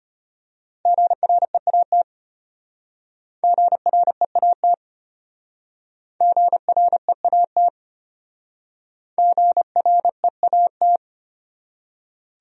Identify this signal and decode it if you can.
Morse code, heard as a keyed tone, about 34 wpm, 700 Hz. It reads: GREAT GREAT GREAT GREAT